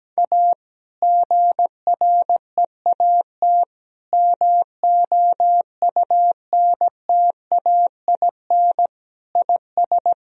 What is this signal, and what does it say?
Morse code, heard as a keyed tone, about 17 wpm, 700 Hz. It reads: A GREAT MOUNTAIN IS
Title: Sentences from Top 1000 Words - Review Part 1 - 17wpm